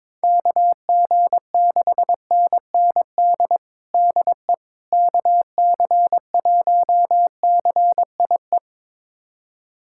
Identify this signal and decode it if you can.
Morse code, heard as a keyed tone, about 21 wpm, 700 Hz. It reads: KG6NND DE KC1CIE
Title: USA Callsign DE Callsign - 6 Characters - 22wpm